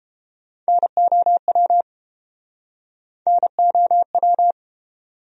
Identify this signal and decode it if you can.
Morse code, heard as a keyed tone, about 30 wpm, 700 Hz. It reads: NOW NOW